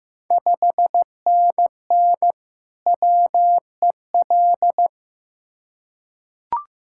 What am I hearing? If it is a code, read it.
Morse code, heard as a keyed tone, about 15 wpm, 700 Hz. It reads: 5NN WEL
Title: Ontario QSO Party Exchange 15wpm